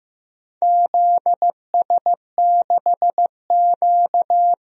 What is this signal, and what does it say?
Morse code, heard as a keyed tone, about 15 wpm, 700 Hz. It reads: ZS6Q
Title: DXCC-100 Call Signs - 4 Characters - Repeated 2x - 15wpm